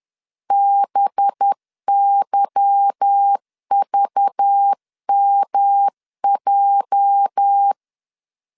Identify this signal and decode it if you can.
Morse code, heard as a keyed tone, about 10 wpm, 800 Hz. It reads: BYVMJ
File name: LEZIONI / 8 YJ45 / 8 YJ4 10wpm 2633